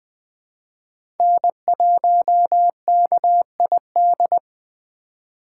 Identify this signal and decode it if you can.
Morse code, heard as a keyed tone, about 20 wpm, 700 Hz. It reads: N1KID
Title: USA Call Signs - 8x Character Spacing - 20wpm